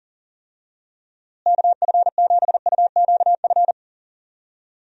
Morse code, heard as a keyed tone, about 37 wpm, 700 Hz. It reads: KF7UQF
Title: USA Call Signs - 6 Characters - Repeated 2x - 40wpm